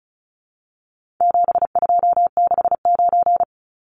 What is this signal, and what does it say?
Morse code, heard as a keyed tone, about 34 wpm, 700 Hz. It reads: 7269
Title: Numbers - 4-Digit 35wpm